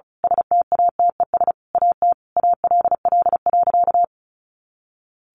Morse code, heard as a keyed tone, about 34 wpm, 700 Hz. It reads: STATES AT ALL.
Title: A Sound Of Thunder 35wpm